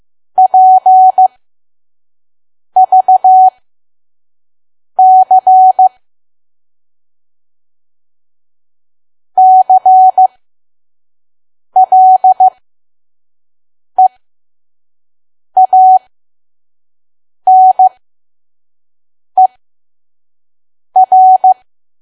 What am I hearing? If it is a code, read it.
Morse code, heard as a keyed tone, about 15 wpm, spaced as for 5 wpm, 750 Hz. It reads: PVC CLEANER
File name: Archive / 5 WPM files / PM